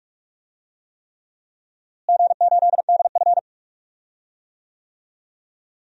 Morse code, heard as a keyed tone, about 46 wpm, 700 Hz. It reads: G8DF
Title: DXCC-100 Call Signs - 4 Characters - Repeated 3x - 45wpm